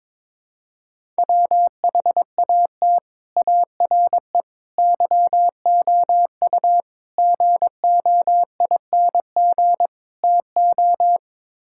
Morse code, heard as a keyed tone, about 22 wpm, 700 Hz. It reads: WHAT ARE YOU GOING TO
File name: practice / 22wpm / Build A Phrase - 5-Word Phrases - 22wpm